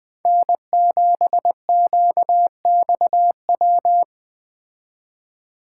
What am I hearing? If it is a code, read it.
Morse code, heard as a keyed tone, about 20 wpm, 700 Hz. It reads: N7QXW
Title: Speed Racing - USA 1x3 Call Signs 17wpm